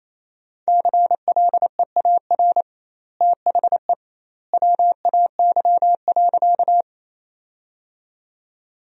Morse code, heard as a keyed tone, about 28 wpm, 700 Hz. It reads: CLEAR THE WAY.
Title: 2x Word-Spacing - Sentences from Top 500 Words - 28wpm